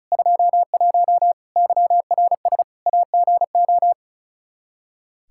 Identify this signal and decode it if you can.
Morse code, heard as a keyed tone, about 34 wpm, 700 Hz. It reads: 21 YRS AGO
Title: Practice QSO - Exchange 3 - More Info - 35wpm